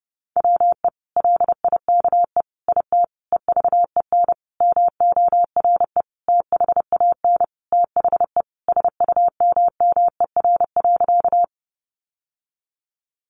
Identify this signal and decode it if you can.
Morse code, heard as a keyed tone, about 30 wpm, 700 Hz. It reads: WE LIKE IT EVEN MORE THAN THE SUMMER.
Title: Sentences from Top 1100 Words 30wpm